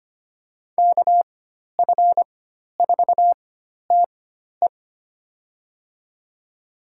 Morse code, heard as a keyed tone, about 25 wpm, spaced as for 12 wpm, 700 Hz. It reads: KF4TE
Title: USA Call Signs - 5 Characters - 4x Character Spacing - 25wpm